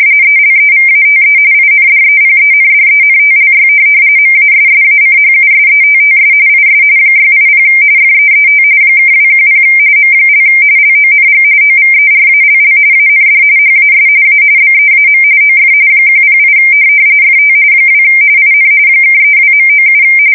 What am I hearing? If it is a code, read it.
RTTY, about 45.45 baud. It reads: ELEVATE INDEPENDENTLY MASSACHUSETTS IS 680 FEET (210 M) LONG OVERALL AND HAS A BEAM OF 108 FT 2 IN (32.97